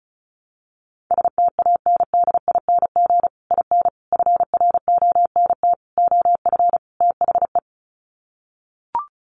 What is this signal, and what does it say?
Morse code, heard as a keyed tone, about 34 wpm, 700 Hz. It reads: STANDING IN FRONT OF THE
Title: Sets of 5 Words Encore2 35wpm